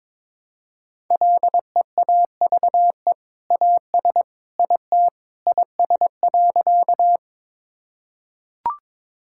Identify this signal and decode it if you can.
Morse code, heard as a keyed tone, about 21 wpm, 700 Hz. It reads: LEAVE AS IT IS.
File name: practice / 22wpm / Sentences with 2-Word Phrases - 22wpm